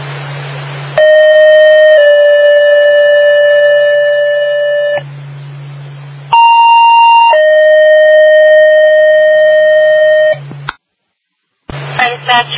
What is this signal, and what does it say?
Two-tone page, heard as a one-second tone, then a three-second tone, about 620 then 580 Hz, about 930 then 620 Hz. It is heard from 1.0 to 10.3 s.